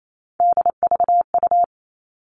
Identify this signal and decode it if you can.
Morse code, heard as a keyed tone, about 28 wpm, 700 Hz. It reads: DVU
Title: All Cut Numbers - 3-Digit 28wpm